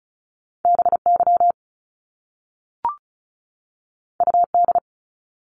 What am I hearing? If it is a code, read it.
Morse code, heard as a keyed tone, about 36 wpm, 700 Hz. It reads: BY UD
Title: Top Two Letter Combinations in Words 35wpm